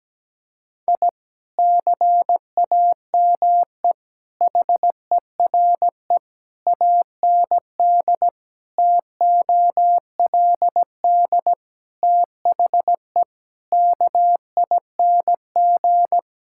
Morse code, heard as a keyed tone, about 17 wpm, 700 Hz. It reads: I CAME HERE AND TOLD THE KING
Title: Sentences from Top 1000 Words - Review Part 6 - 17wpm